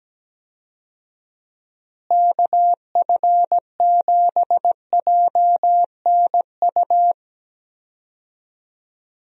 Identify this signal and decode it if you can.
Morse code, heard as a keyed tone, about 17 wpm, 700 Hz. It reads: KF7JNU
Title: USA Call Signs - 6 Characters - Repeated 2x - 17wpm